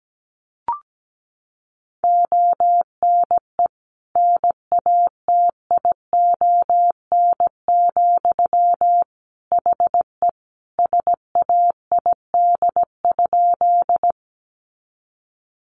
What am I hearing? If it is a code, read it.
Morse code, heard as a keyed tone, about 17 wpm, 700 Hz. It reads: ONE NATION, HE SAID?
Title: Sentences from Top 900 Words 17wpm